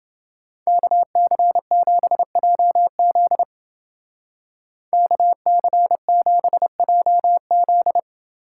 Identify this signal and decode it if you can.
Morse code, heard as a keyed tone, about 29 wpm, 700 Hz. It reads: KC7JZ KC7JZ